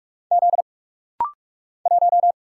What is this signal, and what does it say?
Morse code, heard as a keyed tone, about 44 wpm, 700 Hz. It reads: Z 1
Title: Single Letter-Number - Rapid-Fire 45wpm